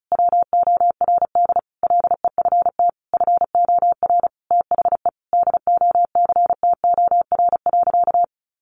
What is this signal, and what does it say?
Morse code, heard as a keyed tone, about 34 wpm, 700 Hz. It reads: WORD LEFT FOR THE DOCTOR.
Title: Sentences from Top 1500 Words 35wpm